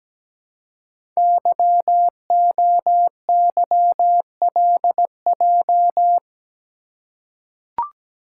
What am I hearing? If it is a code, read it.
Morse code, heard as a keyed tone, about 17 wpm, 700 Hz. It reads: YOYLJ
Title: Code Groups - 5 Letter:Numbers - 17wpm